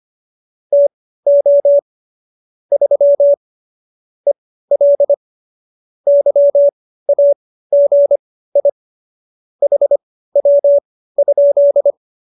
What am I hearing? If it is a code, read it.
Morse code, heard as a keyed tone, about 25 wpm, spaced as for 15 wpm, 570 Hz. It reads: TO 3 EL YAGI HW?